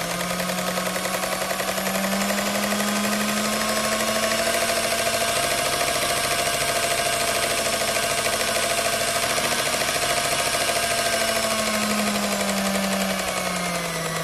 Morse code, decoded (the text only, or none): none